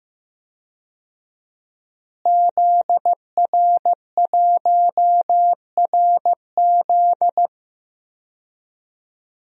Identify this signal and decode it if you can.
Morse code, heard as a keyed tone, about 15 wpm, 700 Hz. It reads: ZR1RZ
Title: DXCC-100 Call Signs - 5 Characters - Repeated 3x - 15wpm